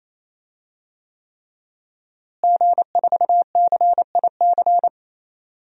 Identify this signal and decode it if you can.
Morse code, heard as a keyed tone, about 27 wpm, 700 Hz. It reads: G4CIC
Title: DXCC-100 Call Signs - 5 Characters - Repeated 3x - 28wpm